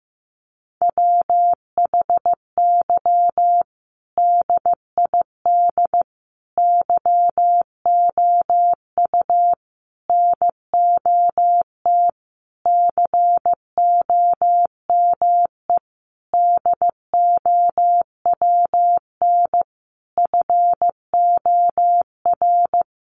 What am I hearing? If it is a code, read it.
Morse code, heard as a keyed tone, about 15 wpm, 700 Hz. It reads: WHY DID YOU NOT COME DOWN FOR
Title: Sentences from Top 1500 Words 15wpm